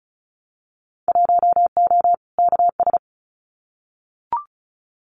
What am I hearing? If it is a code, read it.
Morse code, heard as a keyed tone, about 36 wpm, 700 Hz. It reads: 1O KS
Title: Winter Field Day Exchanges 35wpm